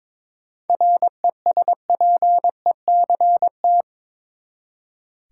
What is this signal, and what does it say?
Morse code, heard as a keyed tone, about 22 wpm, 700 Hz. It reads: RESPECT
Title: Seven Letter Words - 22wpm